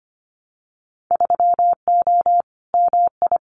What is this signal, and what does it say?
Morse code, heard as a keyed tone, about 25 wpm, 700 Hz. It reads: 3O MI
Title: Winter Field Day Exchanges 25wpm